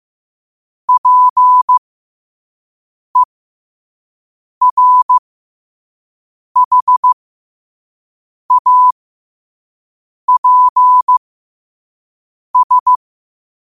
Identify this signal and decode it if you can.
Morse code, heard as a keyed tone, about 15 wpm, spaced as for 5 wpm, 1000 Hz. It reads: PERHAPS
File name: qotdfiles / 5wpm